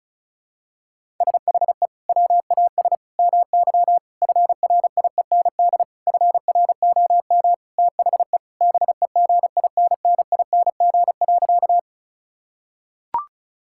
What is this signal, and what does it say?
Morse code, heard as a keyed tone, about 34 wpm, 700 Hz. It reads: SHE WAS MY FRIEND FROM THE BEGINNING.
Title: Sentences from Top 1000 Words - Review Part 5 - 35wpm